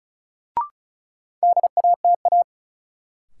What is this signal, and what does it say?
Morse code, heard as a keyed tone, about 34 wpm, 700 Hz. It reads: DATA
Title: Sentences from Top 1000 Words - Review Part 3 - New Word Drill - 35wpm